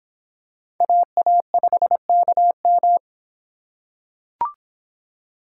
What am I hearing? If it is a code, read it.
Morse code, heard as a keyed tone, about 26 wpm, 700 Hz. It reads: AA5KM